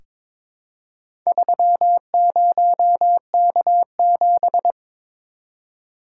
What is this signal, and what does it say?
Morse code, heard as a keyed tone, about 22 wpm, 700 Hz. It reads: 30K7
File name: practice / 22wpm / Four Letter-Number Combinations - 22wpm